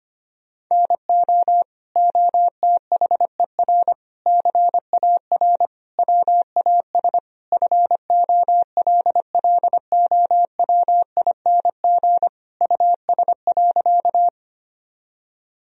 Morse code, heard as a keyed tone, about 25 wpm, 700 Hz. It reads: NO OTHER CAR WAS FOLLOWING US.
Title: Sentences from Top 1000 Words - Review Part 4 - 25wpm